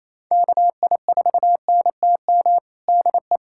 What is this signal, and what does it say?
Morse code, heard as a keyed tone, about 28 wpm, 700 Hz. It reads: KI4NTM DE
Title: USA Callsign DE Callsign 28wpm